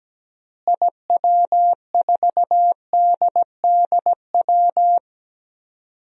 Morse code, heard as a keyed tone, about 17 wpm, 700 Hz. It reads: IW4DDW
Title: DXCC-100 Call Signs - 6 Characters - Repeated 2x - 17wpm